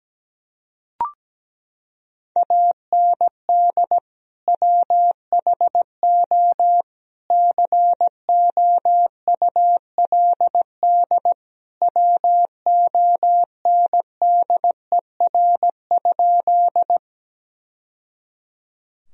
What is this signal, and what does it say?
Morse code, heard as a keyed tone, about 17 wpm, 700 Hz. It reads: AND WHO COULD WONDER?
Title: Sentences from Top 1000 Words - Review Part 3 - 17wpm